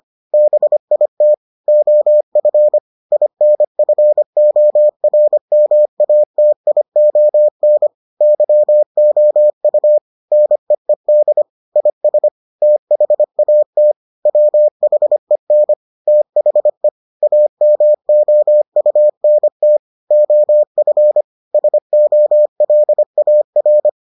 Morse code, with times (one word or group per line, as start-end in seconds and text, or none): 0.3-1.3 BIT
1.7-2.8 OF
3.1-7.9 INFORMATION
8.2-10.0 YOU
10.3-11.4 NEED
11.8-12.3 IS
12.6-13.9 THAT
14.3-15.7 WHEN
16.1-16.9 THE
17.2-19.8 AMOUNT
20.1-21.2 OF
21.5-23.9 SOLAR